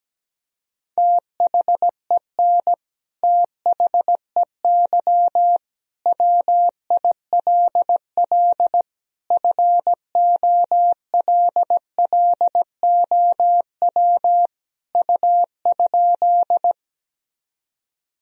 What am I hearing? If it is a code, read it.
Morse code, heard as a keyed tone, about 17 wpm, 700 Hz. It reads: THEN THEY WILL FOLLOW U?
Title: Sentences from Top 200 Words and Abbreviations 17wpm